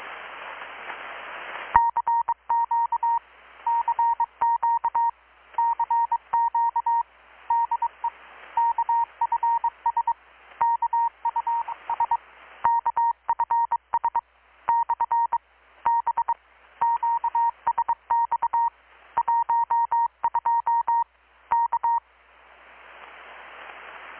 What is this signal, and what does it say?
Morse code, heard as a keyed tone, about 22 wpm, 930 Hz. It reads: CQ CQ CQ DE KFS KUS KFS / B QSX 12 K
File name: blog / swl / 6-abril-2019-12695-USB-KPS